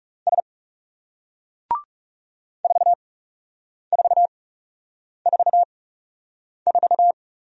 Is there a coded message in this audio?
Yes: Morse code from 0.3 to 7.1 s, heard as a keyed tone, about 39 wpm, spaced as for 8 wpm, 700 Hz. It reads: S 4444